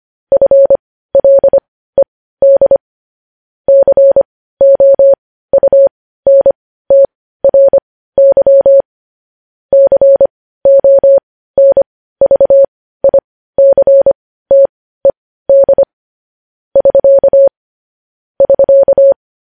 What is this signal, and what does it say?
Morse code, heard as a keyed tone, about 25 wpm, spaced as for 15 wpm, 550 Hz. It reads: FLED COUNTRY CONVICTED <SK> <SK>
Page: news-headlines-in-morse-code-at-15-wpm